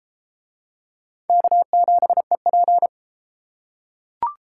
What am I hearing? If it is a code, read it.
Morse code, heard as a keyed tone, about 31 wpm, 700 Hz. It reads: K7EP